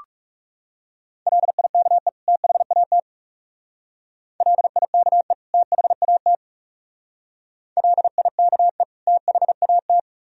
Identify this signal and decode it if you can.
Morse code, heard as a keyed tone, about 36 wpm, 700 Hz. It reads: LIKE THAT LIKE THAT LIKE THAT
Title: Speed-Racing - Sets of 2 Words 30wpm